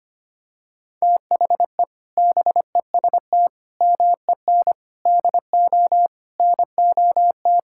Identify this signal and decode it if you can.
Morse code, heard as a keyed tone, about 25 wpm, 700 Hz. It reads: THE BEST MEN DO NOT
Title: Sentences from Top 1000 Words - Review Part 6 - 25wpm